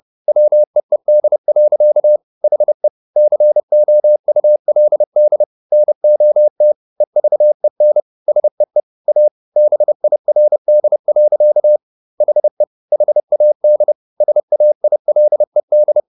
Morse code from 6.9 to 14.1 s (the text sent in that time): EVEN SEE A BIRD. HE HAD